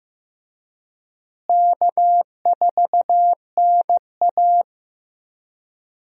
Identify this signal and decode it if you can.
Morse code, heard as a keyed tone, about 15 wpm, 700 Hz. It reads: K4NA
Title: USA Call Signs - 4 Characters - Repeated 2x - 15wpm